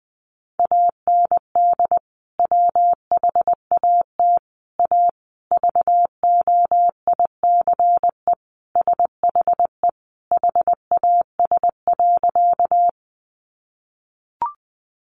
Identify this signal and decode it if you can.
Morse code, heard as a keyed tone, about 19 wpm, 700 Hz. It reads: AND WHAT A VOICE SHE HAS.
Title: Sentences from Top 500 Words 20wpm